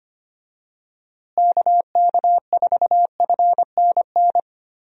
Morse code, heard as a keyed tone, about 25 wpm, 700 Hz. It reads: KK4FNN
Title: USA Call Signs - 6 Characters - Repeated 2x - 25wpm